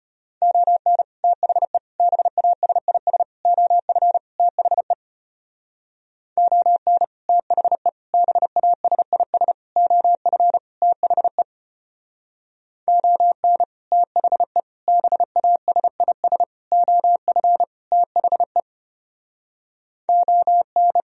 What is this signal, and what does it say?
Morse code, heard as a keyed tone, about 32 wpm, 700 Hz. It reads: ON THE BASIS OF THE ON THE BASIS OF THE ON THE BASIS OF THE ON